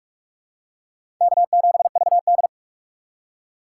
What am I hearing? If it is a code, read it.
Morse code, heard as a keyed tone, about 44 wpm, 700 Hz. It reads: K7VD
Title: USA Call Signs - 4 Characters - 6x Character Spacing - 45wpm